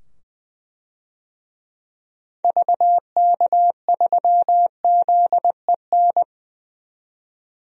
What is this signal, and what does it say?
Morse code, heard as a keyed tone, about 20 wpm, 700 Hz. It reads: VK3ZEN
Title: DXCC-100 Call Signs - 6 Characters - 20wpm